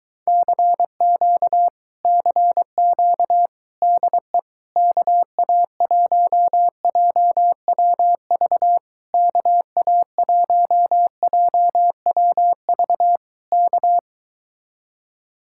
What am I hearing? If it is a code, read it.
Morse code, heard as a keyed tone, about 22 wpm, 700 Hz. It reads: CQ CQ DE KA1JWV KA1JWV K